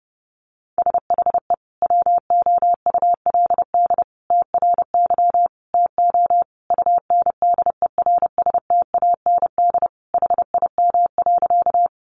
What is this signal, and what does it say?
Morse code, heard as a keyed tone, about 30 wpm, 700 Hz. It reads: SHE WOULD TRY TO UNDERSTAND HIM.
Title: Sentences from Top 600 Words 30wpm